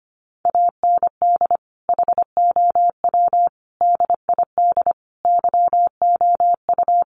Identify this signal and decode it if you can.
Morse code, heard as a keyed tone, about 25 wpm, 700 Hz. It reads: AND HOW DID YOU
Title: Sentences from Top 1400 Words 25wpm